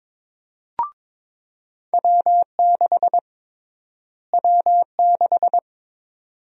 Morse code, heard as a keyed tone, about 22 wpm, 700 Hz. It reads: W6 W6
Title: USA Call Sign Prefixes - Repeated 2x - 22wpm